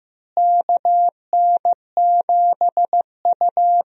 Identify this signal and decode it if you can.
Morse code, heard as a keyed tone, about 15 wpm, 700 Hz. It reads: KN7U
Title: USA Call Signs - 4 Characters - Repeated 3x - 15wpm